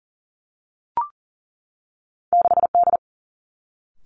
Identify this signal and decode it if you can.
Morse code, heard as a keyed tone, about 38 wpm, 700 Hz. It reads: 6D
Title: Two Challenging Letter-Number Combinations 40wpm